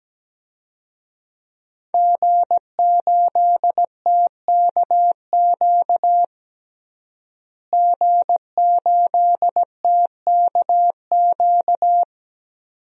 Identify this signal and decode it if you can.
Morse code, heard as a keyed tone, about 17 wpm, 700 Hz. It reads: G8TKQ G8TKQ
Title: DXCC-100 Call Signs - 5 Characters - Repeated 2x - 17wpm